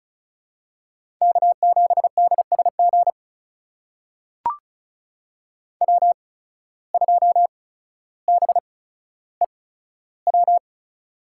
Morse code, heard as a keyed tone, about 34 wpm, spaced as for 15 wpm, 700 Hz. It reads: K7DSG W2BEW
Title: USA Call Signs - 5 Characters - 8x Character Spacing - 35wpm